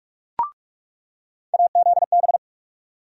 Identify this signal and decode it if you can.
Morse code, heard as a keyed tone, about 45 wpm, 700 Hz. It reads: AZB
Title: USA Call Sign Postfixes - 3 Letters 45wpm